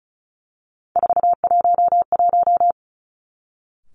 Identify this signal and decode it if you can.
Morse code, heard as a keyed tone, about 36 wpm, 700 Hz. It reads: 411
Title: Three Challenging Letter-Number Combinations 35wpm